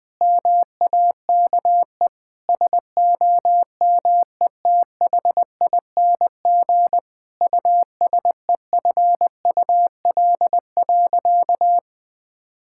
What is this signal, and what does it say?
Morse code, heard as a keyed tone, about 20 wpm, 700 Hz. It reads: MAKE SOMETHING USEFUL.
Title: Sentences from Top 1000 Words - Review Part 5 - 20wpm